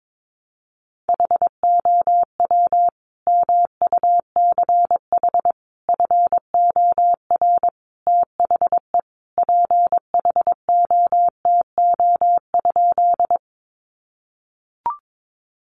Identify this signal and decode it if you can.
Morse code, heard as a keyed tone, about 22 wpm, 700 Hz. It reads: HOW MUCH FOR THE PHOTO?
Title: Sentences from Top 1200 Words 22wpm